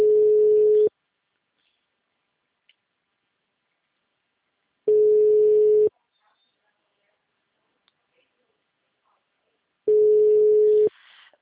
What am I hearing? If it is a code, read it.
Ringback tone (Europe, 425 Hz).